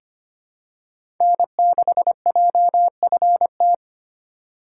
Morse code, heard as a keyed tone, about 25 wpm, 700 Hz. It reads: N6JFT
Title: USA Call Signs - 4x Character Spacing - 25wpm